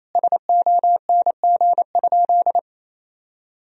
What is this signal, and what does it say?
Morse code, heard as a keyed tone, about 28 wpm, 700 Hz. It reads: SONG?
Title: Sentences from Top 1000 Words - Review Part 2 - 28wpm